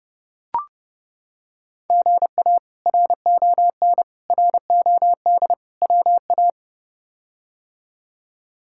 Morse code, heard as a keyed tone, about 30 wpm, 700 Hz. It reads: GA RON ROD WA
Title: K1USN SST QSO Practice - 30wpm